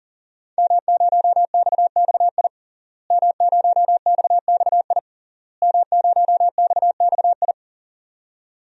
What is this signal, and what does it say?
Morse code, heard as a keyed tone, about 40 wpm, 700 Hz. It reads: M0XXI M0XXI M0XXI
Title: DXCC-100 Call Signs - 5 Characters - Repeated 3x - 40wpm